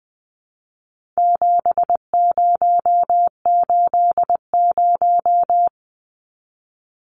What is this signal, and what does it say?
Morse code, heard as a keyed tone, about 20 wpm, 700 Hz. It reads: 7080
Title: Numbers - 4-Digit 20wpm